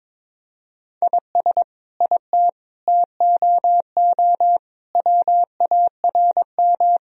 Morse code, heard as a keyed tone, about 22 wpm, 700 Hz. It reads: IS IT TOO WARM
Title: Sentences from Top 1000 Words - Review Part 2 - 22wpm